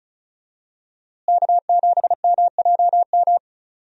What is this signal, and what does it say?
Morse code, heard as a keyed tone, about 36 wpm, 700 Hz. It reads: K7MJM
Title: USA Call Signs - 5 Characters - Repeated 3x - 35wpm